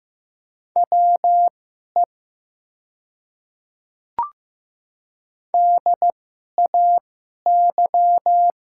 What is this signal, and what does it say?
Morse code, heard as a keyed tone, about 15 wpm, spaced as for 11 wpm, 700 Hz. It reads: WE DAY
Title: 2x Character Spacing - Top 100 Words - 15wpm